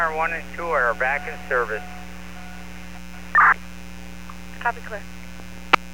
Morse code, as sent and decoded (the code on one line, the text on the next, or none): none